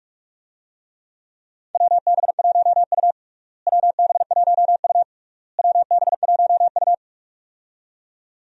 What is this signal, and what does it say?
Morse code, heard as a keyed tone, about 43 wpm, 700 Hz. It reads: WB1U WB1U WB1U
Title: USA Call Signs - 4 Characters - Repeated 3x - 45wpm